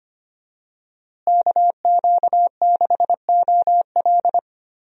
Morse code, heard as a keyed tone, about 25 wpm, 700 Hz. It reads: KQ6OL
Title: USA Call Signs - 5 Characters - Repeated 3x - 25wpm